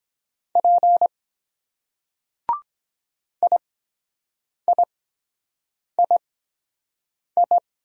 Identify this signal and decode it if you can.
Morse code, heard as a keyed tone, about 22 wpm, spaced as for 7 wpm, 700 Hz. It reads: P IIII